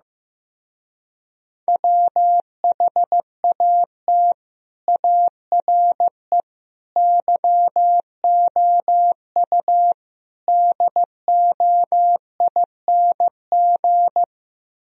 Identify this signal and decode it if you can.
Morse code, heard as a keyed tone, about 15 wpm, 700 Hz. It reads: WHAT ARE YOU DOING
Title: Speed-Racing - Sets of 4 Words 15wpm